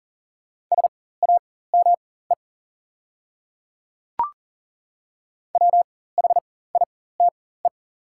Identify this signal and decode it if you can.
Morse code, heard as a keyed tone, about 39 wpm, spaced as for 19 wpm, 690 Hz. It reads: SAME WHITE
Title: 4x Character Spacing - Top 300 Words - 40wpm